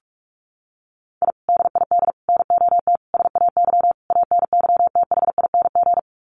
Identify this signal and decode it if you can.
Morse code, heard as a keyed tone, about 44 wpm, 690 Hz. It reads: I DID NOT SAY ANYTHING
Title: Sets of 5 Words Encore2 45wpm